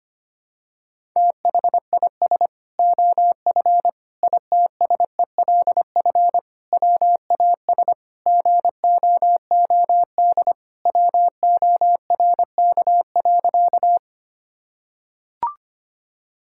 Morse code, heard as a keyed tone, about 25 wpm, 700 Hz. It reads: THIS OF ITSELF WAS GOOD WORK.
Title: Sentences from Top 1000 Words - Review Part 5 - 25wpm